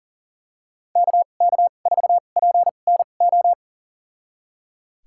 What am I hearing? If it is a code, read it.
Morse code, heard as a keyed tone, about 37 wpm, spaced as for 28 wpm, 700 Hz. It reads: KK4PNO
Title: USA Call Signs - 6 Characters - 2x Character Spacing - 40wpm